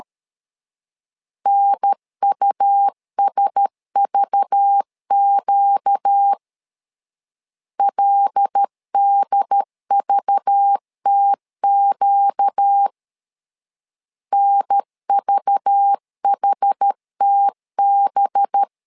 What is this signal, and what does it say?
Morse code, heard as a keyed tone, about 13 wpm, 800 Hz. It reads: NUSVQ LDVTQ NVHTB
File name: LEZIONI / 5 LBVC / 5 LBV 12wpm 1855